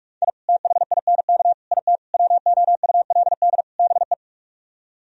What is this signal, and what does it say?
Morse code, heard as a keyed tone, about 50 wpm, 700 Hz. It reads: I THINK IT WOULD BE